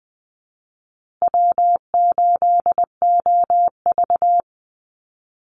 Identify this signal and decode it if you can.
Morse code, heard as a keyed tone, about 20 wpm, 700 Hz. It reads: W8OV
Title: Easy Call Signs Repeated 2x - 20wpm